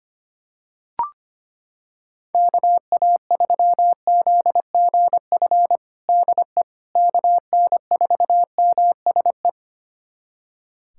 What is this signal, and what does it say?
Morse code, heard as a keyed tone, about 25 wpm, 700 Hz. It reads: KA3ZGF DE KN4MSE
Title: USA Callsign DE Callsign 25wpm